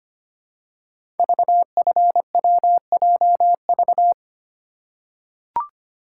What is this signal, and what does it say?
Morse code, heard as a keyed tone, about 25 wpm, 700 Hz. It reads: VFWJV
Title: Code Groups - 5 Letters - 25wpm